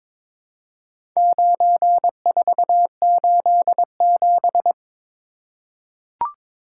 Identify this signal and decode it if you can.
Morse code, heard as a keyed tone, about 23 wpm, 700 Hz. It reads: 9487
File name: practice / 22wpm / Four Letter-Number Combinations - 22wpm